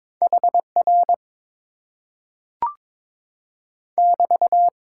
Morse code, heard as a keyed tone, about 22 wpm, 700 Hz. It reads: HR <BT>
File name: Top 90 QSO Elements - 8x Character Spacing - 22wpm